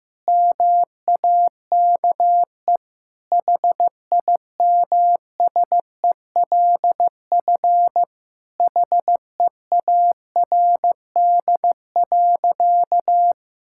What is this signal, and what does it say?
Morse code, heard as a keyed tone, about 16 wpm, 700 Hz. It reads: MAKE HIMSELF HEARD.
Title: Sentences from Top 1600 Words 15wpm